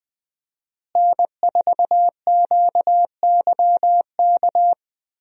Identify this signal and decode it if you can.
Morse code, heard as a keyed tone, about 20 wpm, 700 Hz. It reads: N4QYK
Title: USA Call Signs - 5 Characters - Repeated 2x - 20wpm